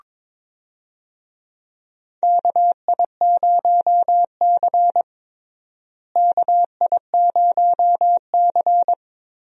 Morse code, heard as a keyed tone, about 22 wpm, 700 Hz. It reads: KI0C KI0C
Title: USA Call Signs - 4 Characters - Repeated 2x - 22wpm